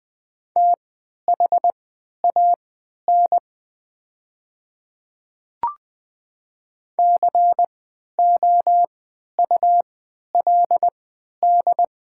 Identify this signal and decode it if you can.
Morse code, heard as a keyed tone, about 20 wpm, spaced as for 11 wpm, 690 Hz. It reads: THAN COULD
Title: 3x Character Spacing - Top 100 Words - 20wpm